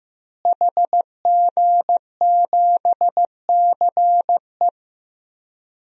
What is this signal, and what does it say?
Morse code, heard as a keyed tone, about 15 wpm, 700 Hz. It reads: HG7CE
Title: DXCC-100 Call Signs - 5 Characters - Repeated 3x - 15wpm